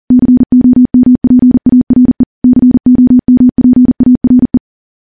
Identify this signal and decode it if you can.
Morse code, heard as a keyed tone, about 39 wpm, 250 Hz. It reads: COMPARE COMPARE
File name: morse / morse-training-40WPM